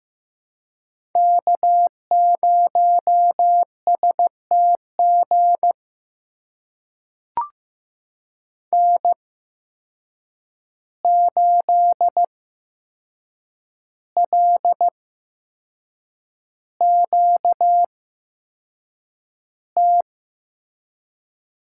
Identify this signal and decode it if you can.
Morse code, heard as a keyed tone, about 15 wpm, spaced as for 6 wpm, 700 Hz. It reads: K0STG N8LQT
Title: USA Call Signs - 8x Character Spacing - 15wpm